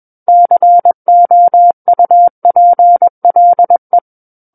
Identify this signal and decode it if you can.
Morse code, heard as a keyed tone, about 21 wpm, 700 Hz. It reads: COUPLE
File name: Book One / 12-Chapter 12 - 21 WPM